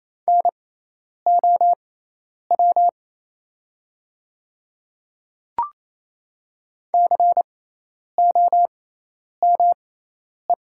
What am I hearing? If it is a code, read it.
Morse code, heard as a keyed tone, about 28 wpm, spaced as for 10 wpm, 700 Hz. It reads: NOW COME
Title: 6x Character Spacing - Top 100 Words - 28wpm